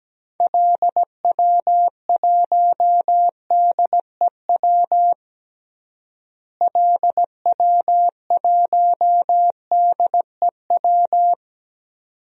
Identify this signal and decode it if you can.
Morse code, heard as a keyed tone, about 17 wpm, 700 Hz. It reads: LW1DEW LW1DEW
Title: DXCC-100 Call Signs - 6 Characters - Repeated 2x - 17wpm